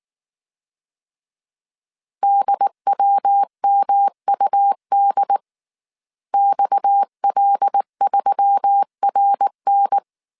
Morse code, heard as a keyed tone, about 19 wpm, 800 Hz. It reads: DWMUD XL3RN
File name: CW / DET / 1 6 1500C S A 18wpm 2255